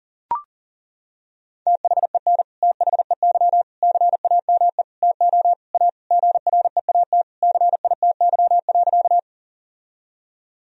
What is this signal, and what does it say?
Morse code, heard as a keyed tone, about 40 wpm, 700 Hz. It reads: THEN THEY CAME TO A GREAT CITY.
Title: Sentences from Top 1000 Words - Review Part 1 - 40wpm